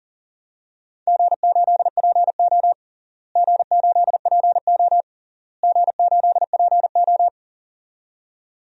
Morse code, heard as a keyed tone, about 40 wpm, 700 Hz. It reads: G8PO G8PO G8PO
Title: DXCC-100 Call Signs - 4 Characters - Repeated 3x - 40wpm